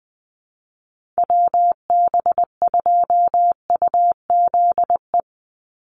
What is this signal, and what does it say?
Morse code, heard as a keyed tone, about 20 wpm, 700 Hz. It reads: WB2UZE
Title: Call Signs - 20wpm